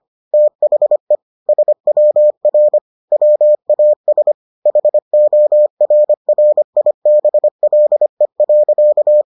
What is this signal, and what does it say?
Morse code, heard as a keyed tone, about 25 wpm, 600 Hz. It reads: THE SWR WAS HORRIBLE.